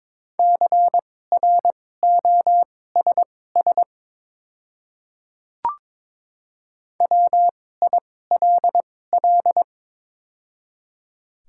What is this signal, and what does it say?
Morse code, heard as a keyed tone, about 22 wpm, spaced as for 16 wpm, 700 Hz. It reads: CROSS WILL
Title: 2x Character Spacing - Top 500 Words - 22wpm